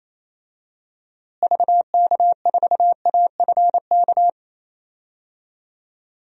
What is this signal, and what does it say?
Morse code, heard as a keyed tone, about 28 wpm, 700 Hz. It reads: VK4AFK
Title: DXCC-100 Call Signs - 6 Characters - Repeated 3x - 28wpm